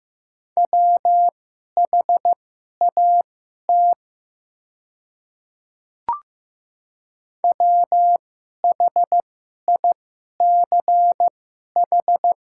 Morse code, heard as a keyed tone, about 15 wpm, spaced as for 11 wpm, 700 Hz. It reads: WHAT WHICH
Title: 2x Character Spacing - Top 100 Words - 15wpm